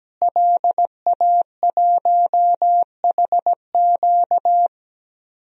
Morse code, heard as a keyed tone, about 17 wpm, 700 Hz. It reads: LA1HQ
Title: DXCC-100 Call Signs - 5 Characters - Repeated 3x - 17wpm